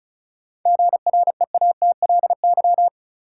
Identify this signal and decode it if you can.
Morse code, heard as a keyed tone, about 34 wpm, 700 Hz. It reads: GREATLY
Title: Sentences from Top 1400 Words - New Word Drill - 35wpm